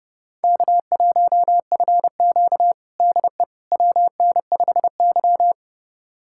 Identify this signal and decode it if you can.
Morse code, heard as a keyed tone, about 30 wpm, 700 Hz. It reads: K1FQ DE WN5Y
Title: USA Callsign DE Callsign - 4 Characters - 30wpm